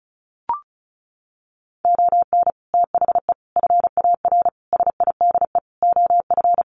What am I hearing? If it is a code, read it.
Morse code, heard as a keyed tone, about 34 wpm, 700 Hz. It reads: ON THE FAR SIDE OF
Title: Sets of 5 Words 35wpm